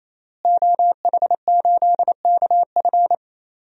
Morse code, heard as a keyed tone, about 28 wpm, 700 Hz. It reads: OH8KF
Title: DXCC-100 Call Signs - 5 Characters - Repeated 3x - 28wpm